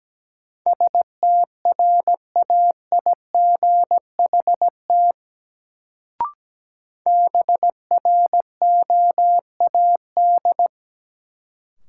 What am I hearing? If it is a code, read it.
Morse code, heard as a keyed tone, about 17 wpm, 700 Hz. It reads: STRAIGHT BROAD
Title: Sentences from Top 1000 Words - New Word Drill - 17wpm